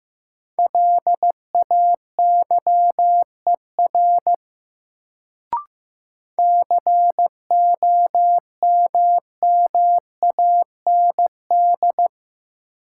Morse code, heard as a keyed tone, about 15 wpm, 700 Hz. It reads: LAYER COMMAND
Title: Sentences from Top 1600 Words - New Word Drill Alt - 15wpm